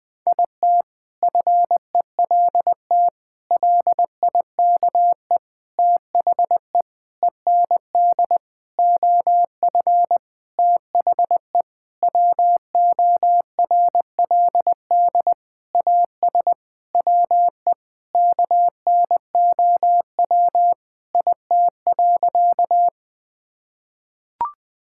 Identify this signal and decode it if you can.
Morse code, heard as a keyed tone, about 20 wpm, 700 Hz. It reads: IT FELT LIKE THE END OF THE WORLD AS WE KNOW IT.
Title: Sentences with 8-Word Phrases - 20wpm